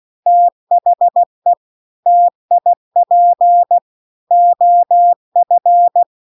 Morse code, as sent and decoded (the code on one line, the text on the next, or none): - .... / - .. .--. / --- ..-.
THE TIP OF